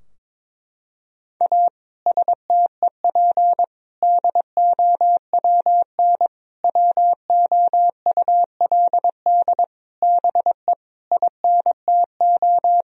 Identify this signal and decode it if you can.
Morse code, heard as a keyed tone, about 21 wpm, 700 Hz. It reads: A STEP DOWN WOULD BE INTO
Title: Sentences from Top 1000 Words - Review Part 4 - 22wpm